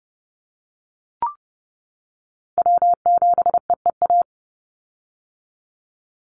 Morse code, heard as a keyed tone, about 30 wpm, 700 Hz. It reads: W7EEA
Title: USA Call Signs 30wpm